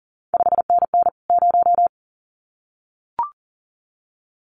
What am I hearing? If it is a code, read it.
Morse code, heard as a keyed tone, about 38 wpm, 700 Hz. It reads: 5NN 0
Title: CQ WW DX Contest Exchange Encore - 40wpm